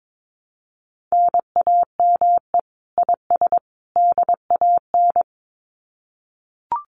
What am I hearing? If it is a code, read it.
Morse code, heard as a keyed tone, about 22 wpm, 700 Hz. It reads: NAME IS DAN
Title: Common QSO Phrases 22wpm